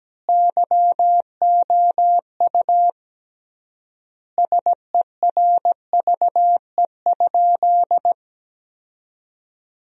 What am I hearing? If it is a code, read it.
Morse code, heard as a keyed tone, about 17 wpm, 700 Hz. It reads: YOU SERVE?
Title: 3x Word-Spacing - Sentences from Top 400 Words - 17wpm